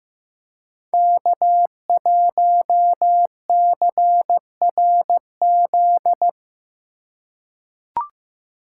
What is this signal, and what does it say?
Morse code, heard as a keyed tone, about 15 wpm, 700 Hz. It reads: K1CRZ
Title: USA Call Signs - 3x Character Spacing - 15wpm